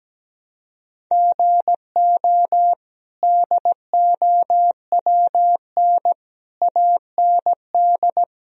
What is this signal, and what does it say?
Morse code, heard as a keyed tone, about 17 wpm, 700 Hz. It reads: GO DOWN AND
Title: Sentences from Top 1000 Words - Review Part 1 - 17wpm